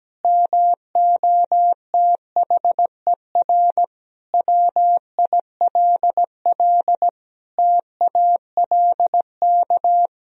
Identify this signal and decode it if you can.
Morse code, heard as a keyed tone, about 17 wpm, 700 Hz. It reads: MOTHER WILL TALK
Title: Sentences from Top 1000 Words - Review Part 1 - 17wpm